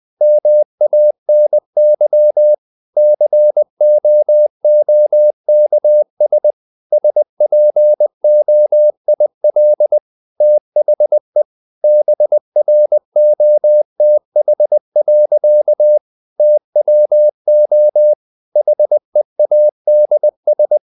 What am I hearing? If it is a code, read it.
Morse code, heard as a keyed tone, about 20 wpm, 600 Hz. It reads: MANY COOKS SPOIL THE BROTH. TWO HEADS
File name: CW / Practice / EnglishSayings20